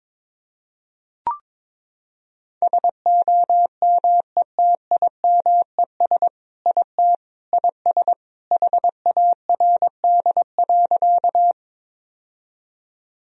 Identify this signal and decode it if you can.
Morse code, heard as a keyed tone, about 22 wpm, 700 Hz. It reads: SOMETIMES IT IS HARD.
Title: Sentences from Top 1000 Words - Review Part 3 - 22wpm